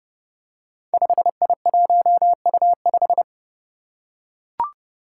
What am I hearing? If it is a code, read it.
Morse code, heard as a keyed tone, about 30 wpm, 700 Hz. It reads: 5I1U5
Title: Five Challenging Letter-Number Combinations 30wpm